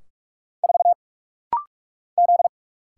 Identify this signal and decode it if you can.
Morse code, heard as a keyed tone, about 44 wpm, 690 Hz. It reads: 4 Z
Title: Single Letter-Number - Rapid-Fire 45wpm